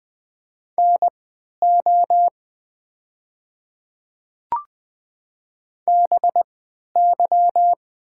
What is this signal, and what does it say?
Morse code, heard as a keyed tone, about 20 wpm, spaced as for 11 wpm, 700 Hz. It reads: NO BY
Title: 3x Character Spacing - Top 100 Words - 20wpm